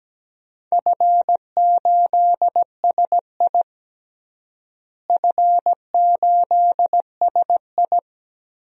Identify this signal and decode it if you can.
Morse code, heard as a keyed tone, about 17 wpm, 700 Hz. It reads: F8SI F8SI
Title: DXCC-100 Call Signs - 4 Characters - Repeated 3x - 17wpm